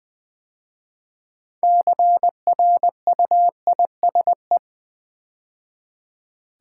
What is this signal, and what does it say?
Morse code, heard as a keyed tone, about 20 wpm, 700 Hz. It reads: CRUISE
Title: 6 Letter Words - First 3 Letters Given - 20wpm